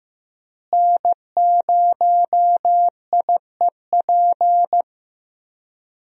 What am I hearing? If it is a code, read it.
Morse code, heard as a keyed tone, about 15 wpm, 700 Hz. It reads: N0IEP
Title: USA Call Signs - 5 Characters - Repeated 3x - 15wpm